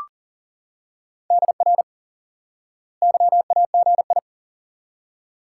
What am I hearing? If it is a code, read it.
Morse code, heard as a keyed tone, about 37 wpm, 700 Hz. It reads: DR YAGI
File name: IQE Course 3 - Lesson 89 - YAGI - Two Elements - 40wpm